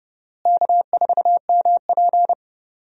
Morse code, heard as a keyed tone, about 30 wpm, 700 Hz. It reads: K4MP
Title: USA Call Signs - 4 Characters - Repeated 3x - 30wpm